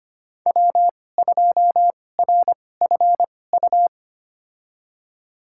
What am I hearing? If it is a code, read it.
Morse code, heard as a keyed tone, about 25 wpm, spaced as for 18 wpm, 700 Hz. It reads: W2RFU
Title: USA Call Signs - 5 Characters - 2x Character Spacing - 25wpm